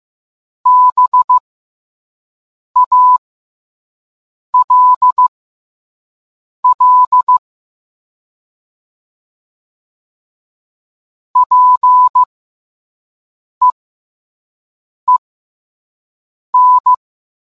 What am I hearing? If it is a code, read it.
Morse code, heard as a keyed tone, about 15 wpm, spaced as for 5 wpm, 1000 Hz. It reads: BALL PEEN